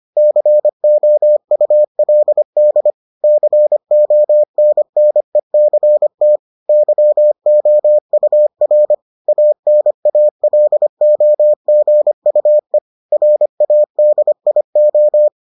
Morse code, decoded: COULD CONNECT YOUR ANALOGUE RADIO

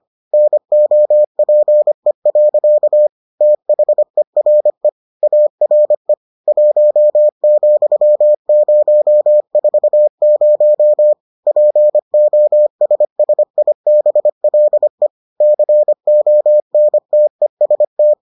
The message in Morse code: NOPE. THERE ARE 1,040 POSSIBLE CONTEST

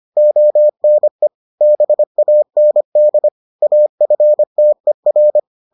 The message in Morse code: ONE BAND AFTER

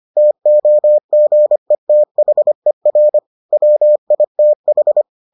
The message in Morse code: TOGETHER WITH

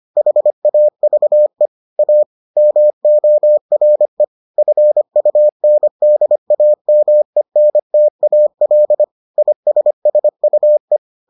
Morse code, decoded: HAVE A MORE FUNDAMENTAL ISSUE

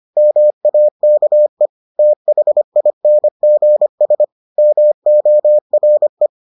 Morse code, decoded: MAKE THINGS MORE

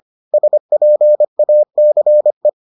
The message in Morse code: SPACE

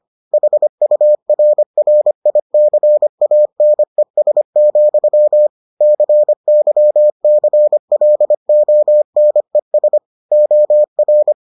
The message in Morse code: HURRICANES, CYCLONES OR

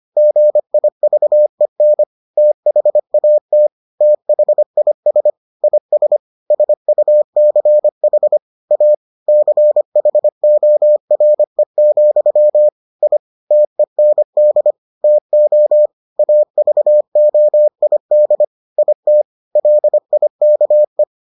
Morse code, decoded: GIVEN THAT THIS IS SUCH A CHORE, I TEND TO AVOID IT LIKE

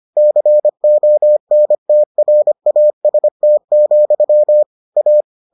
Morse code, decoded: CONTRAST, A

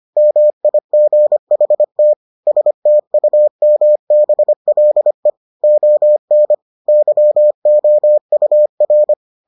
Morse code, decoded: MIGHT STUMBLE ON YOUR

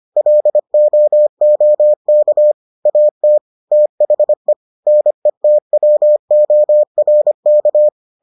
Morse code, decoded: LOOK AT THE NETWORK

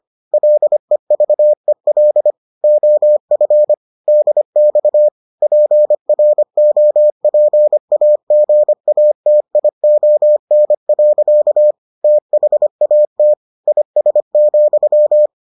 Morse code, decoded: LEVEL OF DX PROPAGATION. THAT IS,

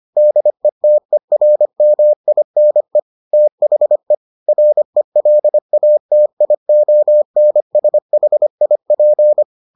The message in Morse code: DETERMINE THE RELATIONSHIP

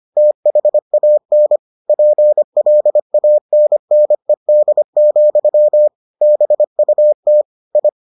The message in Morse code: THAN PLANNED, BUT I

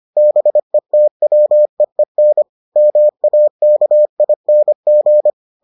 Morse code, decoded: BETWEEN MAKING

